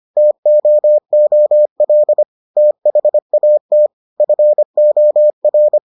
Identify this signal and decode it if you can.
Morse code, heard as a keyed tone, about 25 wpm, 600 Hz. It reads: TOOL THAT FOR